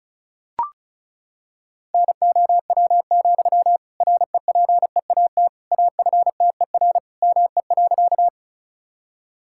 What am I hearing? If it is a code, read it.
Morse code, heard as a keyed tone, about 36 wpm, 700 Hz. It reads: NOW, REPEAT AFTER ME.
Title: Sentences from Top 1600 Words 35wpm